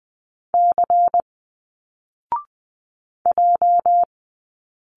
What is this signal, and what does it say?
Morse code, heard as a keyed tone, about 20 wpm, 700 Hz. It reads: C J